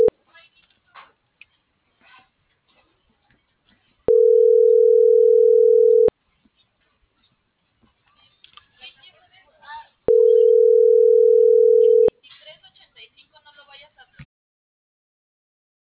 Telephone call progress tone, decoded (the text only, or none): ringback tone